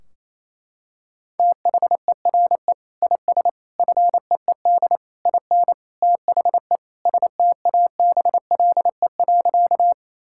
Morse code, heard as a keyed tone, about 28 wpm, 700 Hz. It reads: THERE IS FEED IN THE STABLE.
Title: Sentences from Top 1600 Words 28wpm